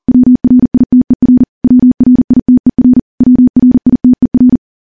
Morse code, heard as a keyed tone, about 40 wpm, 250 Hz. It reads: WRITER WRITER WRITER
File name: morse / morse-training-40WPM